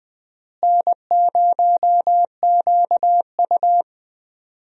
Morse code, heard as a keyed tone, about 20 wpm, 700 Hz. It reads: N0QU
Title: USA Call Signs - 4 Characters - Repeated 2x - 20wpm